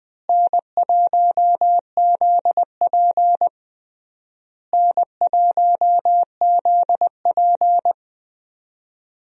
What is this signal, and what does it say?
Morse code, heard as a keyed tone, about 20 wpm, 700 Hz. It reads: N1ZP N1ZP
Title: USA Call Signs - 4 Characters - Repeated 3x - 20wpm